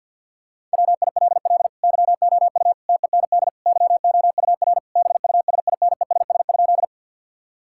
Morse code, heard as a keyed tone, about 50 wpm, 700 Hz. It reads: WILL YOU TEND YOUR BUSINESS?
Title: Sentences from Top 1000 Words - Review Part 5 - 50wpm